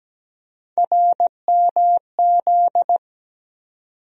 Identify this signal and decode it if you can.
Morse code, heard as a keyed tone, about 17 wpm, 700 Hz. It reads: RMZ
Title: USA Call Sign Postfixes - 3 Letters 17wpm